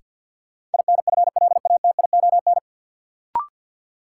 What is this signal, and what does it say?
Morse code, heard as a keyed tone, about 50 wpm, 700 Hz. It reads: INFLATION
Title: Sentences from Top 1100 Words - New Word Drill - 50wpm